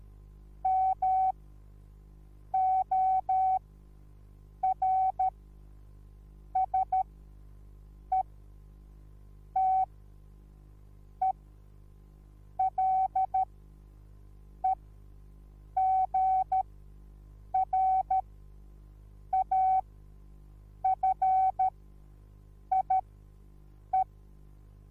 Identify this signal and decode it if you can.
Morse code, heard as a keyed tone, about 13 wpm, spaced as for 6 wpm, 760 Hz. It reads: MORSETELEGRAFIE
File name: Onlinelehrgang / morsen / DJ4UF Morsen21